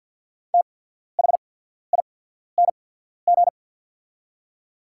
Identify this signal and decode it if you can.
Morse code, heard as a keyed tone, about 48 wpm, spaced as for 14 wpm, 700 Hz. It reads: THING
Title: 8x Character Spacing - Top 100 Words - 50wpm